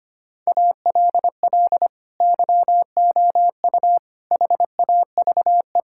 Morse code, heard as a keyed tone, about 25 wpm, 700 Hz. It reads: ALL YOU HAVE
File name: practice / 25wpm / Sentences with 9-Word Phrases - 25wpm